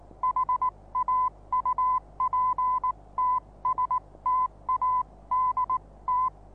Morse code, heard as a keyed tone, about 19 wpm, spaced as for 17 wpm, 1010 Hz. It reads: HAUPTSTADT
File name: repos / Morsekurs / DJ4UF Morsen26